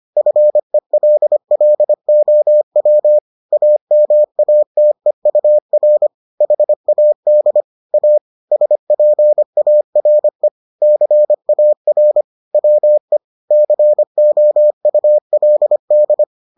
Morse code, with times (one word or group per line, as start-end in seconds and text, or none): 0.2-3.2 FELLOW
3.5-6.1 AMATEUR
6.4-7.6 HAD
7.9-8.2 A
8.5-10.5 SPARE
10.8-12.2 CAR
12.5-13.2 WE
13.5-16.2 COULD